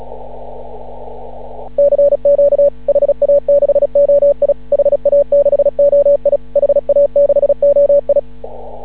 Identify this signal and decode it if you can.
Morse code, heard as a keyed tone, about 36 wpm, 560 Hz. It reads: CQ HA6OI HA6OI HA6OI
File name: CW / ha6oi 2